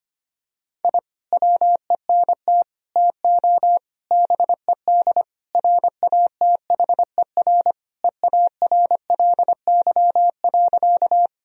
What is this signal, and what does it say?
Morse code, heard as a keyed tone, about 25 wpm, 700 Hz. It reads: I WENT TO BED RATHER EARLY.
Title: Sentences from Top 1000 Words - Review Part 4 - 25wpm